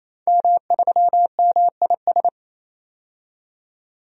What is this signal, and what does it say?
Morse code, heard as a keyed tone, about 27 wpm, 700 Hz. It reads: M3MIS
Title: DXCC-100 Call Signs - 5 Characters - 28wpm